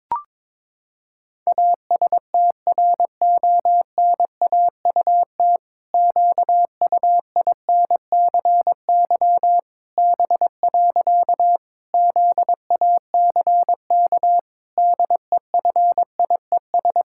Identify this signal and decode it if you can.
Morse code, heard as a keyed tone, about 22 wpm, 700 Hz. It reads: ASTRONAUT QUINCY B. ZACK DEFIES
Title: Pangram Challenge v2 - 22wpm